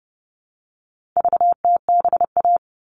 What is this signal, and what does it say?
Morse code, heard as a keyed tone, about 30 wpm, 700 Hz. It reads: VTBA
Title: All Cut Numbers - 4-Digit 30wpm